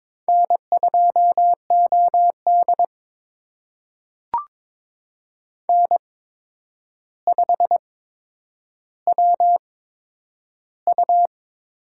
Morse code, heard as a keyed tone, about 22 wpm, spaced as for 9 wpm, 700 Hz. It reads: N2OD N5WU
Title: USA Call Signs - 4 Characters - 8x Character Spacing - 22wpm